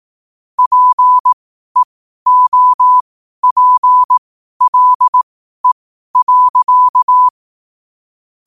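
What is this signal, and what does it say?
Morse code, heard as a keyed tone, about 18 wpm, spaced as for 12 wpm, 1000 Hz. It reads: PEOPLE.
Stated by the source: Morse code quote-of-the-day practice at 12 words per minute.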